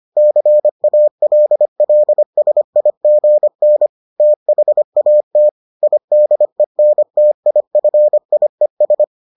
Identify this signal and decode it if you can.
Morse code, heard as a keyed tone, about 25 wpm, 600 Hz. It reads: CALLSIGN THAT IDENTIFIES